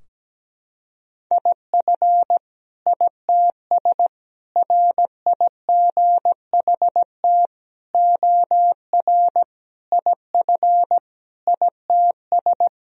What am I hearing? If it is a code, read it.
Morse code, heard as a keyed tone, about 17 wpm, 700 Hz. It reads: IF ITS RIGHT OR IF ITS
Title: Sentences from Top 1000 Words - Review Part 3 - 17wpm